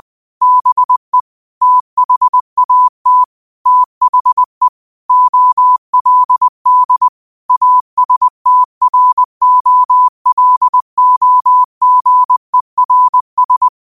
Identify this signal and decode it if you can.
Morse code, heard as a keyed tone, about 20 wpm, 1000 Hz. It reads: BE THAT THE OLD ASTROLOGERS
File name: qotdfiles / 20wpm